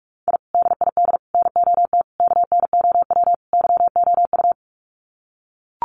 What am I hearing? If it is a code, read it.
Morse code, heard as a keyed tone, about 44 wpm, 700 Hz. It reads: I DID NOT KNOW YOU
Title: Sets of 5 Words Encore2 45wpm